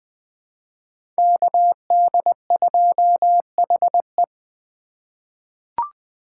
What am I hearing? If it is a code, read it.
Morse code, heard as a keyed tone, about 20 wpm, 700 Hz. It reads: KD2HE
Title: K1USN SST QSO Practice - 20wpm